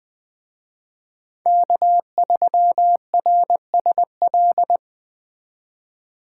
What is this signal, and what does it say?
Morse code, heard as a keyed tone, about 20 wpm, 700 Hz. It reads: K3RSL
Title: Speed Racing - USA 1x3 Call Signs 17wpm